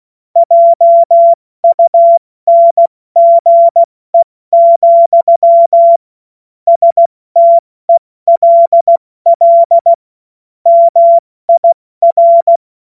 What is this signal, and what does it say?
Morse code, heard as a keyed tone, about 16 wpm, spaced as for 14 wpm, 680 Hz. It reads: JUNGE, STELL MIR
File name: Morsekurs / Witze / Witz 09 Januar 2024 70 F80